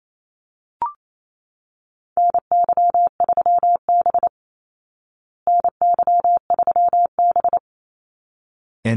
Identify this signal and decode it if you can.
Morse code, heard as a keyed tone, about 28 wpm, 700 Hz. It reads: NY3B NY3B
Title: Call Signs Repeated 3x - 28wpm